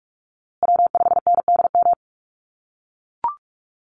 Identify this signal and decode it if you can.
Morse code, heard as a keyed tone, about 44 wpm, 690 Hz. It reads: W5NDM
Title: USA Call Signs Repeated 2x 45wpm